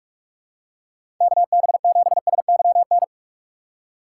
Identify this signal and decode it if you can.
Morse code, heard as a keyed tone, about 41 wpm, 700 Hz. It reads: KB7SYN
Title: USA Call Signs - 6 Characters - 3x Character Spacing - 45wpm